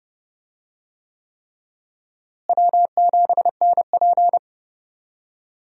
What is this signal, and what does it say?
Morse code, heard as a keyed tone, about 32 wpm, 700 Hz. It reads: W7NP
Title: USA Call Signs - 4 Characters - Repeated 3x - 30wpm